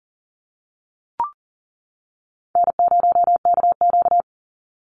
Morse code, heard as a keyed tone, about 40 wpm, 700 Hz. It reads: N0KQ
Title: Easy Call Signs - 40wpm